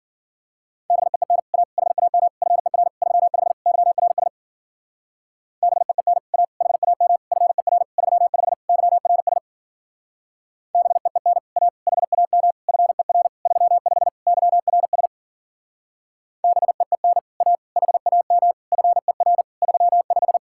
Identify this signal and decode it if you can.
Morse code, heard as a keyed tone, about 50 wpm, 700 Hz. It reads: BEEN A HAM FER 35 YRS BEEN A HAM FER 35 YRS BEEN A HAM FER 35 YRS BEEN A HAM FER 35
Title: Speed-Racing Common QSO Phrases Encore - 40wpm